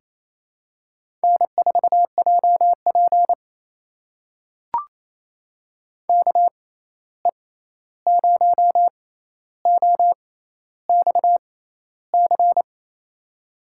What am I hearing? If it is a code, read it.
Morse code, heard as a keyed tone, about 28 wpm, spaced as for 13 wpm, 700 Hz. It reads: N4JP KE0OXC
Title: USA Call Signs - 6x Character Spacing - 28wpm